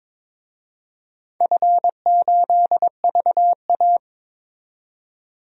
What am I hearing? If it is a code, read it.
Morse code, heard as a keyed tone, about 22 wpm, 700 Hz. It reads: F8VA
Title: DXCC-100 Call Signs - 4 Characters - 22wpm